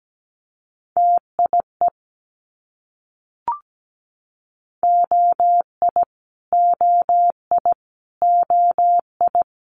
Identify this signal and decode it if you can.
Morse code, heard as a keyed tone, about 17 wpm, 700 Hz. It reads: TIE OI OI OI
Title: Multi-letter Phonemes Repeated 3x 17wpm